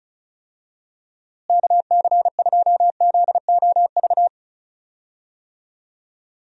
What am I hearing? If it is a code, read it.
Morse code, heard as a keyed tone, about 36 wpm, 700 Hz. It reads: KC2ZOV
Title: USA Call Signs - 6 Characters - Repeated 2x - 35wpm